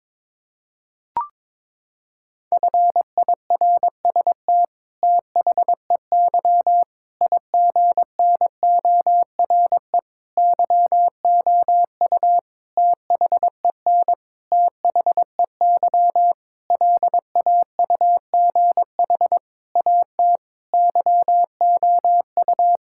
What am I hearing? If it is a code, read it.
Morse code, heard as a keyed tone, about 22 wpm, 700 Hz. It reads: FIRST THEY IGNORE YOU THEN THEY LAUGH AT YOU
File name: practice / 22wpm / Sentences with 8-Word Phrases - 22wpm